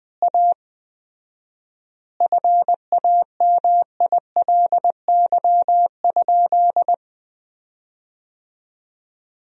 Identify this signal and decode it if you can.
Morse code, heard as a keyed tone, about 21 wpm, 700 Hz. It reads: A FAMILY?
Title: 4x Word-Spacing - Sentences from Top 400 Words - 20wpm